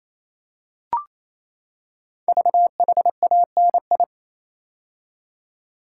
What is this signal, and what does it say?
Morse code, heard as a keyed tone, about 27 wpm, 700 Hz. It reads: VHANI
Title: Code Groups - 5 Letters - 28wpm